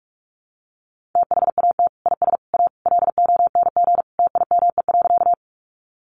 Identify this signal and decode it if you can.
Morse code, heard as a keyed tone, about 44 wpm, 700 Hz. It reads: THAT IS A LONG TIME.
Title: Sentences from Top 100 Words 45wpm